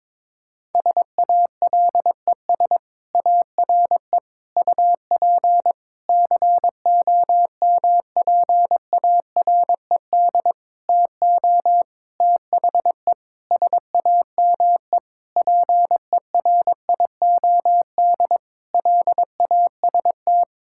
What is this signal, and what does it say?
Morse code, heard as a keyed tone, about 22 wpm, 700 Hz. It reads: SALES ARE UP COMPARED TO THE SAME PERIOD LAST
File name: Sentences with 7-Word Phrases - 22wpm